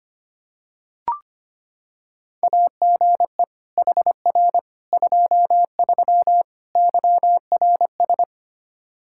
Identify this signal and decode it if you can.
Morse code, heard as a keyed tone, about 25 wpm, 700 Hz. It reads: AGE HR 23 YRS
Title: Practice QSO - Exchange 3 - More Info - 25wpm